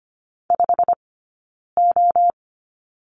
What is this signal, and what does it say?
Morse code, heard as a keyed tone, about 25 wpm, 700 Hz. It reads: 5 O
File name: Lesson 15d - Number 5 - Two Characters 25wpm